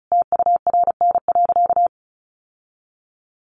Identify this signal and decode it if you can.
Morse code, heard as a keyed tone, about 34 wpm, 700 Hz. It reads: TURN.
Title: Sentences from Top 700 Words 35wpm